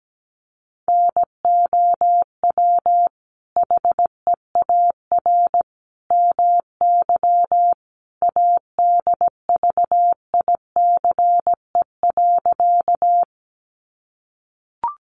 Morse code, heard as a keyed tone, about 17 wpm, 700 Hz. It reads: NOW HEAR MY ADVICE.
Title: Sentences from Top 1100 Words 17wpm